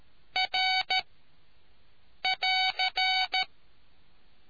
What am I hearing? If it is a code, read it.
Morse code, heard as a keyed tone, about 13 wpm, 750 Hz. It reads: R <AR>